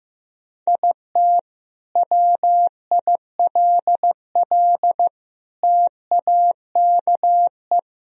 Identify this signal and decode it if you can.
Morse code, heard as a keyed tone, about 15 wpm, 700 Hz. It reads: IT WILL TAKE
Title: Sentences from Top 1000 Words - Review Part 4 - 15wpm